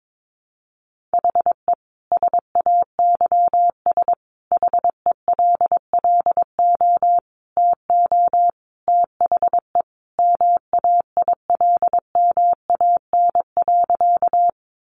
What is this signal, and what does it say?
Morse code, heard as a keyed tone, about 22 wpm, 700 Hz. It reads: HE SAYS HELLO TO THE MAILMAN.
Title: Easy Phrases 22wpm